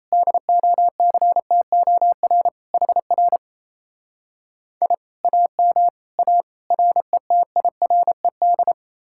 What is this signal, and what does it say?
Morse code, heard as a keyed tone, about 37 wpm, 700 Hz. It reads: DOCTOR HR I AM A RETIRED